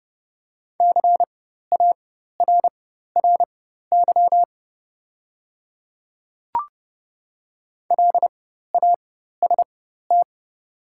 Morse code, heard as a keyed tone, about 30 wpm, spaced as for 14 wpm, 700 Hz. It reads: CARRY LAST
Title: 4x Character Spacing - Top 300 Words - 30wpm